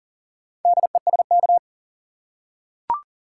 Morse code, heard as a keyed tone, about 40 wpm, 700 Hz. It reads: DESK
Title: Four Letter Words - 40wpm